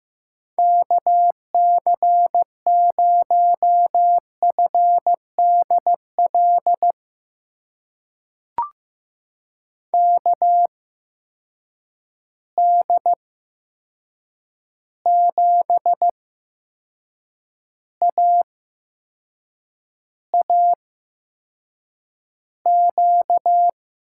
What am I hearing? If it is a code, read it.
Morse code, heard as a keyed tone, about 15 wpm, spaced as for 6 wpm, 700 Hz. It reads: KC0FDL KD7AAQ
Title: USA Call Signs - 6 Characters - 8x Character Spacing - 15wpm